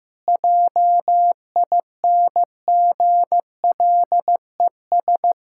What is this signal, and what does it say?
Morse code, heard as a keyed tone, about 16 wpm, 700 Hz. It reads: JINGLES
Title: Pangram Challenge v2 - 15wpm